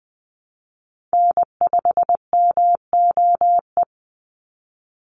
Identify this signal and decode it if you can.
Morse code, heard as a keyed tone, about 20 wpm, 700 Hz. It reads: N5MOE
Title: USA Call Signs 20wpm